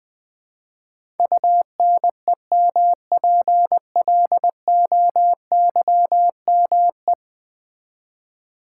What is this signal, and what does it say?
Morse code, heard as a keyed tone, about 20 wpm, 700 Hz. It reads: UNEMPLOYME
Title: Build A Word - 5 Letters At A Time - 12-Letter Words - 20wpm